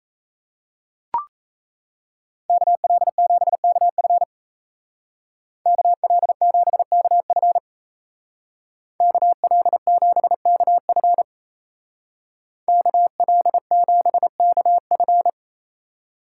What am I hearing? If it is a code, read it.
Morse code, heard as a keyed tone, about 36 wpm, 700 Hz. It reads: KL7KF KL7KF KL7KF KL7KF